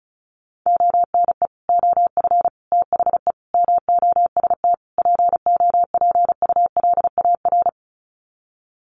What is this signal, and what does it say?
Morse code, heard as a keyed tone, about 34 wpm, 700 Hz. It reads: ONE OF THE MOST POPULAR
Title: Sets of 5 Words 35wpm